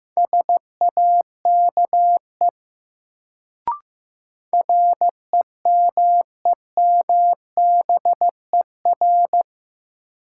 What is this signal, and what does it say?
Morse code, heard as a keyed tone, about 15 wpm, 700 Hz. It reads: SAKE REMEMBER
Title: Sets of 4 Words - New Word Drill - 15wpm